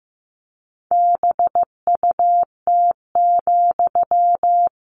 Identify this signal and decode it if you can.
Morse code, heard as a keyed tone, about 15 wpm, 700 Hz. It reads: BUT,
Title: Sentences from Top 600 Words 15wpm